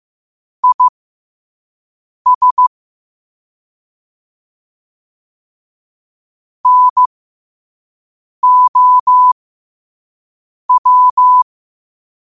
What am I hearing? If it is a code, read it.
Morse code, heard as a keyed tone, about 15 wpm, spaced as for 5 wpm, 1000 Hz. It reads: IS NOW
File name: qotdfiles / 5wpm